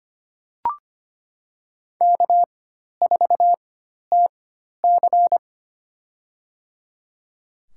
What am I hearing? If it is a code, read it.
Morse code, heard as a keyed tone, about 25 wpm, spaced as for 12 wpm, 700 Hz. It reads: K4TC
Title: USA Call Signs - 4 Characters - 4x Character Spacing - 25wpm